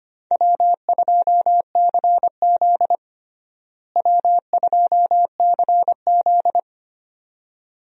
Morse code, heard as a keyed tone, about 25 wpm, 700 Hz. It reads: W2CZ W2CZ
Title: USA Call Signs - 4 Characters - Repeated 2x - 25wpm